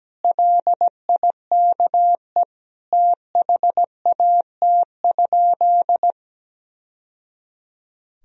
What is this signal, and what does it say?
Morse code, heard as a keyed tone, about 17 wpm, 700 Hz. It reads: LIKE THAT?
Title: Sentences from Top 1000 Words - Review Part 1 - 17wpm